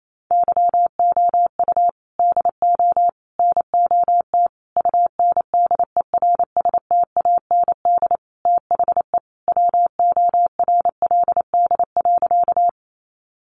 Morse code, heard as a keyed tone, about 28 wpm, 700 Hz. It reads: YOU DO NOT UNDERSTAND THE WORLD.
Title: Sentences from Top 600 Words 28wpm